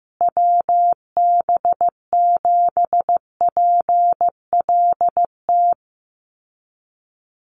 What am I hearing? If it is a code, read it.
Morse code, heard as a keyed tone, about 15 wpm, 700 Hz. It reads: WB7PLT
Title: USA Call Signs Repeated 3x 15wpm